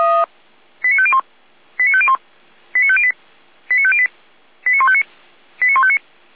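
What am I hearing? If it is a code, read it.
Five-tone selective call (ZVEI), heard as quick runs of five tones, tones of about 70 ms. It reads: 89501 89501 89508 89508 89159 89159
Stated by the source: ZVEI selective call received by SDR, recorded by fire brigade Orlamünde (feuerwehr-orlamuende.org).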